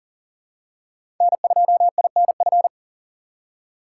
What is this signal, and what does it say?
Morse code, heard as a keyed tone, about 40 wpm, 700 Hz. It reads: N2INF
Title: USA Call Signs - 2x Character Spacing - 40wpm